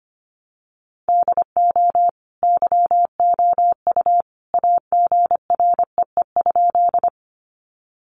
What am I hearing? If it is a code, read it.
Morse code, heard as a keyed tone, about 25 wpm, 700 Hz. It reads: DO YOU AGREE?
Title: Sentences from Top 1100 Words 25wpm